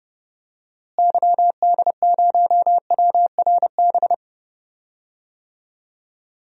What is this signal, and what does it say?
Morse code, heard as a keyed tone, about 30 wpm, 700 Hz. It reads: YD0WRB
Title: DXCC-100 Call Signs - 6 Characters - Repeated 3x - 30wpm